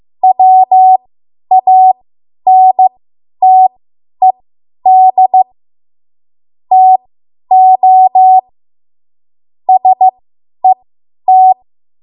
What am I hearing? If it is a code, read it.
Morse code, heard as a keyed tone, about 15 wpm, spaced as for 10 wpm, 750 Hz. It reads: WANTED TO SET
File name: Archive / 10 WPM / 10WPM